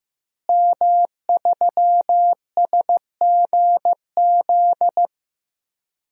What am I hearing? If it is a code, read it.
Morse code, heard as a keyed tone, about 15 wpm, 700 Hz. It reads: M3SGZ
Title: DXCC-100 Call Signs - 5 Characters - Repeated 2x - 15wpm